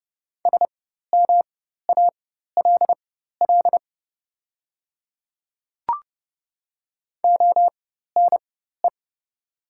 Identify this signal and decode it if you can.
Morse code, heard as a keyed tone, about 30 wpm, spaced as for 14 wpm, 700 Hz. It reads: SMALL ONE
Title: 4x Character Spacing - Top 200 Words - 30wpm